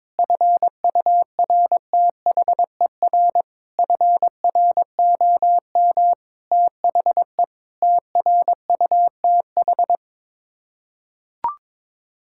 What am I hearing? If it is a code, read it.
Morse code, heard as a keyed tone, about 21 wpm, 700 Hz. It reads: FURTHER FROM THE TRUTH
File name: practice / 22wpm / Sets of 7 Words - 22wpm